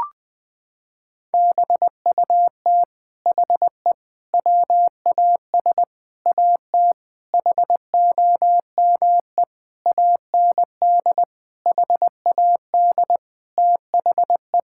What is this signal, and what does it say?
Morse code, heard as a keyed tone, about 21 wpm, 700 Hz. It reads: BUT HE WAS AT HOME AND HAD THE
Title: Sentences from Top 1000 Words - Review Part 5 - 20wpm